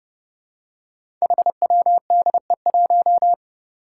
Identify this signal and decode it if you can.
Morse code, heard as a keyed tone, about 30 wpm, 700 Hz. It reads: HWDE1
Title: Five Challenging Letter-Number Combinations 30wpm